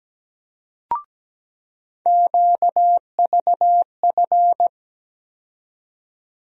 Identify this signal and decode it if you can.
Morse code, heard as a keyed tone, about 17 wpm, 700 Hz. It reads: QVF
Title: USA Call Sign Postfixes - 3 Letters 17wpm